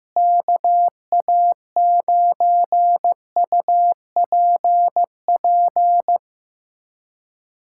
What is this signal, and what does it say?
Morse code, heard as a keyed tone, about 15 wpm, 700 Hz. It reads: KA9UPP
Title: USA Callsign DE Callsign 15wpm